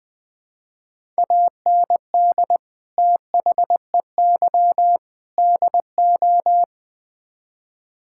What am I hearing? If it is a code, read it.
Morse code, heard as a keyed tone, about 20 wpm, 700 Hz. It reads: AND THEY DO
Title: Build A Phrase - 4-Word Phrases - 20wpm